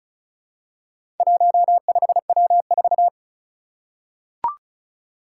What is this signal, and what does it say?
Morse code, heard as a keyed tone, about 36 wpm, 700 Hz. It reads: 15W4
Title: Four Challenging Letter-Number Combinations 35wpm